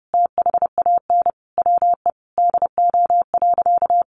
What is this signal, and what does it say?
Morse code, heard as a keyed tone, about 30 wpm, 700 Hz. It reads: THAN WE DO.
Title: Sentences from Top 100 Words 30wpm